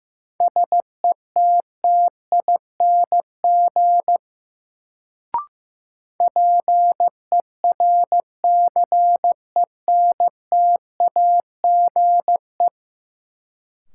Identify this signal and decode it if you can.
Morse code, heard as a keyed tone, about 15 wpm, 700 Hz. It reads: SETTING PERCENTAGE
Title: Sentences from Top 1300 Words - New Word Drill - 15wpm